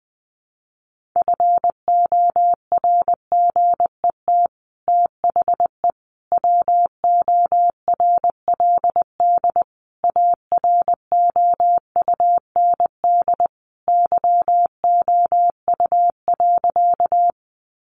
Morse code, heard as a keyed tone, about 20 wpm, 700 Hz. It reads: FORGET THE WORLD AROUND YOU.
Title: Sentences from Top 1200 Words 20wpm